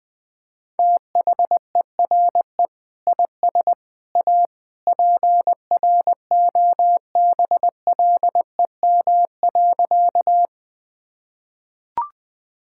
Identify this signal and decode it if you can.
Morse code, heard as a keyed tone, about 20 wpm, 700 Hz. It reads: THERE IS A PROBLEM.
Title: Sentences with 3-Word Phrases - Spoken Beginning - 20wpm